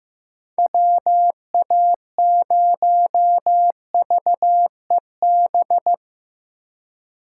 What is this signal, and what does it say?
Morse code, heard as a keyed tone, about 15 wpm, 700 Hz. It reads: WA0VEB
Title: USA Call Signs - 6 Characters - 4x Character Spacing - 15wpm